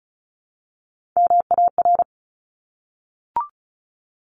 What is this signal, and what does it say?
Morse code, heard as a keyed tone, about 34 wpm, 700 Hz. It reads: MAR
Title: ARRL Sections 35wpm